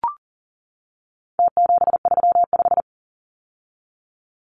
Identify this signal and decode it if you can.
Morse code, heard as a keyed tone, about 40 wpm, 700 Hz. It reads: T735
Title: Numbers - Contest Style - 4-Digit 40wpm